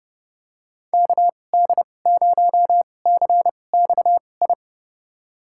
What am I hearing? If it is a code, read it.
Morse code, heard as a keyed tone, about 30 wpm, spaced as for 22 wpm, 700 Hz. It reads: KD0CXI
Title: USA Call Signs - 6 Characters - 2x Character Spacing - 30wpm